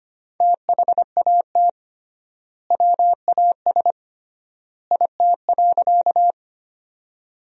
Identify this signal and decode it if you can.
Morse code, heard as a keyed tone, about 25 wpm, 700 Hz. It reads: THAT WAS IT.
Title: 3x Word-Spacing - Sentences from Top 100 Words - 25wpm